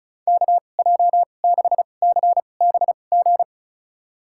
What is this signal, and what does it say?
Morse code, heard as a keyed tone, about 36 wpm, spaced as for 25 wpm, 700 Hz. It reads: KJ6CBG
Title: USA Call Signs - 6 Characters - 2x Character Spacing - 35wpm